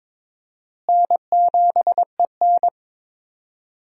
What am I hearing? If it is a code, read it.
Morse code, heard as a keyed tone, about 23 wpm, 700 Hz. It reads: N7EN
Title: USA Call Signs - 4 Characters - Repeated 3x - 22wpm